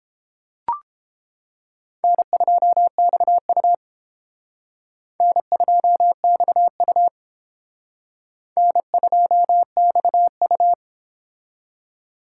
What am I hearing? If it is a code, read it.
Morse code, heard as a keyed tone, about 29 wpm, 700 Hz. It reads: N2XU N2XU N2XU